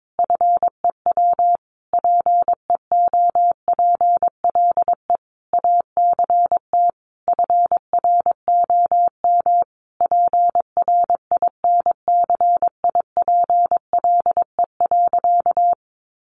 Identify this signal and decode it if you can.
Morse code, heard as a keyed tone, about 22 wpm, 700 Hz. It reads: FEW PEOPLE ACT FROM PRINCIPLE.
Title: Sentences from Top 1400 Words 22wpm